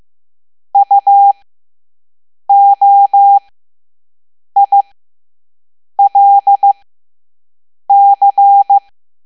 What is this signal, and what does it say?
Morse code, heard as a keyed tone, about 15 wpm, spaced as for 6 wpm, 800 Hz. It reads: UOILC
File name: Morse / Level7-All-A